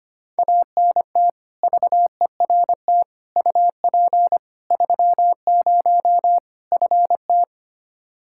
Morse code, heard as a keyed tone, about 25 wpm, 700 Hz. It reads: ANT VERT UP 30 FT
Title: K1USN SST QSO Practice 25wpm